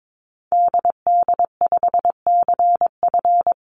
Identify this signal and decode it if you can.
Morse code, heard as a keyed tone, about 22 wpm, 700 Hz. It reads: DD5CF
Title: Call Signs - 22wpm